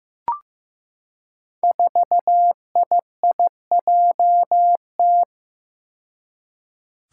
Morse code, heard as a keyed tone, about 15 wpm, 700 Hz. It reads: 4IIJT
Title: Five Challenging Letter-Number Combinations 15wpm